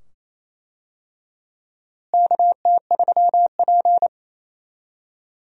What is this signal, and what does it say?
Morse code, heard as a keyed tone, about 28 wpm, 700 Hz. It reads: KT3P
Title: USA Call Signs - 4 Characters - Repeated 2x - 28wpm